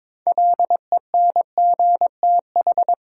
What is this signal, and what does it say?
Morse code, heard as a keyed tone, about 22 wpm, 700 Hz. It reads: LENGTH
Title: Sentences from Top 1000 Words - New Word Drill - 22wpm